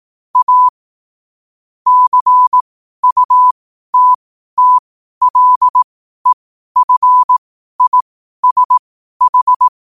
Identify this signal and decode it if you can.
Morse code, heard as a keyed tone, about 18 wpm, spaced as for 12 wpm, 1000 Hz. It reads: A CUTTLEFISH
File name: qotdfiles / 12wpm